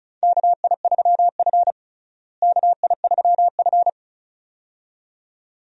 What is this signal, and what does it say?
Morse code, heard as a keyed tone, about 36 wpm, 700 Hz. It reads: KI3F KI3F
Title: USA Call Signs - 4 Characters - Repeated 2x - 35wpm